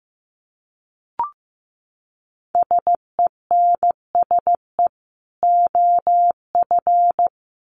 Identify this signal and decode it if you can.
Morse code, heard as a keyed tone, about 15 wpm, 700 Hz. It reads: SENSE OF
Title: Sets of 2 Words 15wpm